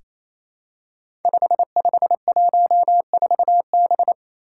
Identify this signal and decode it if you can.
Morse code, heard as a keyed tone, about 27 wpm, 700 Hz. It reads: 5514B
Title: Five Challenging Letter-Number Combinations 28wpm